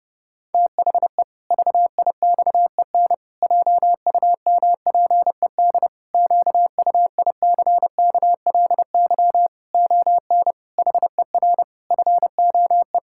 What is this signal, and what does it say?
Morse code, heard as a keyed tone, about 30 wpm, 700 Hz. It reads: THE VIXEN JUMPED QUICKLY ON HER FOE
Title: Pangram Challenge v2 - 30wpm